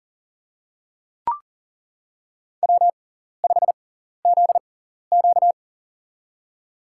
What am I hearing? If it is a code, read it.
Morse code, heard as a keyed tone, about 40 wpm, spaced as for 14 wpm, 700 Hz. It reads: W5ZQ
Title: USA Call Signs - 4 Characters - 6x Character Spacing - 40wpm